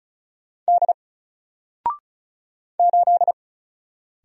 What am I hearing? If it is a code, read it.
Morse code, heard as a keyed tone, about 36 wpm, 700 Hz. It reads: D 8
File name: Single-Letter-Number-35wpm